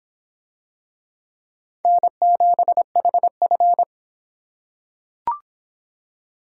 Morse code, heard as a keyed tone, about 26 wpm, 700 Hz. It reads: N7HF